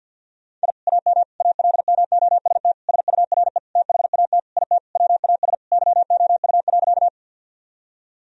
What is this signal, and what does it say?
Morse code, heard as a keyed tone, about 50 wpm, 700 Hz. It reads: I AM ALMOST SURE THAT IT WAS YOU.
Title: Sentences from Top 1000 Words - Review Part 4 - 50wpm